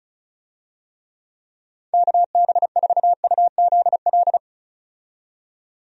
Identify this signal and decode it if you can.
Morse code, heard as a keyed tone, about 34 wpm, 700 Hz. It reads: KB4UZL
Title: USA Call Signs - 6 Characters - 35wpm